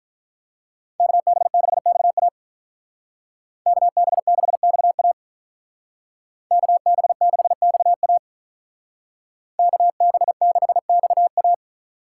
Morse code, heard as a keyed tone, about 44 wpm, 700 Hz. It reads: KB6XA KB6XA KB6XA KB6XA